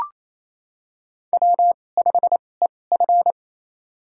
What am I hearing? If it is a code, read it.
Morse code, heard as a keyed tone, about 28 wpm, spaced as for 20 wpm, 700 Hz. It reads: W5EF
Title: USA Call Signs - 4 Characters - 2x Character Spacing - 28wpm